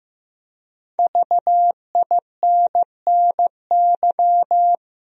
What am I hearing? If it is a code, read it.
Morse code, heard as a keyed tone, about 15 wpm, 700 Hz. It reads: VINNY
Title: USA Common Names - Max 5 Letters - 15wpm